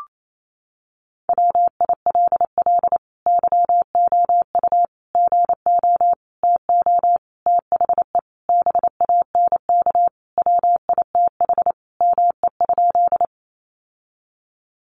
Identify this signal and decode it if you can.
Morse code, heard as a keyed tone, about 28 wpm, 690 Hz. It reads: WILL YOU GO TO THE BANK WITH ME?
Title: Sentences from Top 900 Words 28wpm